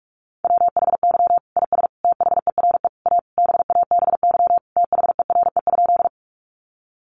Morse code, heard as a keyed tone, about 44 wpm, 700 Hz. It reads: WHY IS THERE A BABY THERE?
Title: Sentences from Top 1300 Words 45wpm